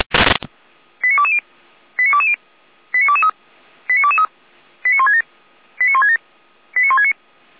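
Five-tone selective call (ZVEI), heard as quick runs of five tones, tones of about 70 ms. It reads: 89220 89220 89223 89223 89167 89167 89169
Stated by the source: ZVEI five-tone call received on an SDR.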